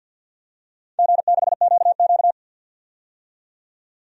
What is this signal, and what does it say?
Morse code, heard as a keyed tone, about 52 wpm, 700 Hz. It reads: G6QQ
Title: DXCC-100 Call Signs - 4 Characters - 50wpm